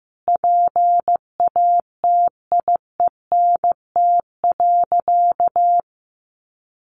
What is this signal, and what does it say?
Morse code, heard as a keyed tone, about 14 wpm, 700 Hz. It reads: PATIENT.
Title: Easy Phrases 15wpm